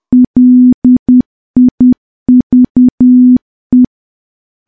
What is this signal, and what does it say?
Morse code, heard as a keyed tone, about 10 wpm, 250 Hz. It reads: LIVE
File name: morse / morse-training-10WPM